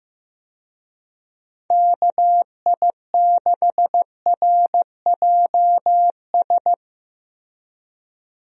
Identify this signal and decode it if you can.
Morse code, heard as a keyed tone, about 15 wpm, 700 Hz. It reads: KI6RJS
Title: USA Call Signs - 6 Characters - Repeated 2x - 15wpm